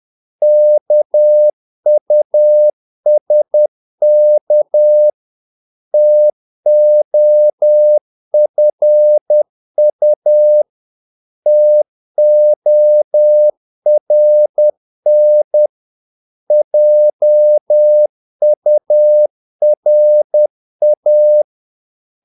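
Morse code, heard as a keyed tone, about 10 wpm, 600 Hz. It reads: KUSK TOFU TORN JURA
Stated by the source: Morse practice of short Danish words at 10 wpm, from CW med Gnister.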